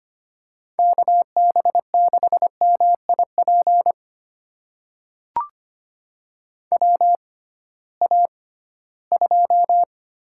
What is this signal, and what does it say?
Morse code, heard as a keyed tone, about 25 wpm, spaced as for 16 wpm, 700 Hz. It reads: KB6MIP WA2
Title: USA Call Signs - 6 Characters - 6x Character Spacing - 25wpm